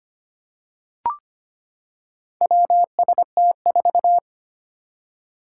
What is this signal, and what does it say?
Morse code, heard as a keyed tone, about 25 wpm, 700 Hz. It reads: WST4
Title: Four Challenging Letter-Number Combinations 25wpm